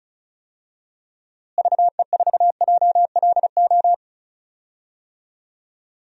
Morse code, heard as a keyed tone, about 36 wpm, 700 Hz. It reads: VE4JLO
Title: DXCC-100 Call Signs - 6 Characters - Repeated 2x - 35wpm